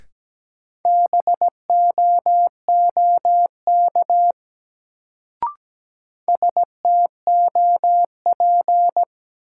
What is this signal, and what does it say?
Morse code, heard as a keyed tone, about 17 wpm, 700 Hz. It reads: BOOK STOP
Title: Sentences from Top 300 Words - New Word Drill - 17wpm